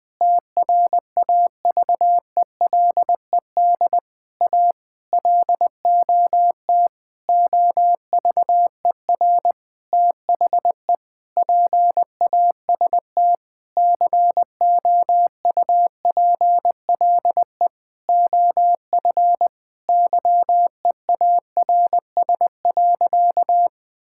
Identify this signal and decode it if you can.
Morse code, heard as a keyed tone, about 20 wpm, 700 Hz. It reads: TRAVELED A LOT OVER THE PAST COUPLE OF YEARS.
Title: Sentences with 6-Word Phrases - Spoken Beginning - 20wpm